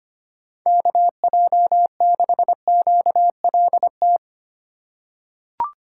Morse code, heard as a keyed tone, about 25 wpm, 700 Hz. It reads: KJ6QLT
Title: USA Call Signs - 6 Characters - 3x Character Spacing - 25wpm